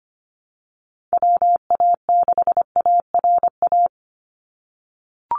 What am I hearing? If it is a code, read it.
Morse code, heard as a keyed tone, about 25 wpm, 700 Hz. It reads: WA6ARA
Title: NA SOTA Chaser Callsigns 25wpm